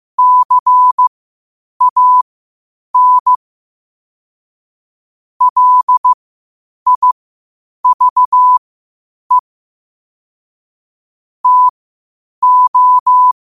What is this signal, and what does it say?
Morse code, heard as a keyed tone, about 15 wpm, spaced as for 8 wpm, 1000 Hz. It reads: CAN LIVE TO
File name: qotdfiles / 8wpm